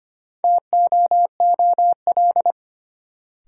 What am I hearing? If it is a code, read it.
Morse code, heard as a keyed tone, about 25 wpm, 700 Hz. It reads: TOOL
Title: Sentences from Top 1000 Words - Review Part 6 - New Word Drill - 25wpm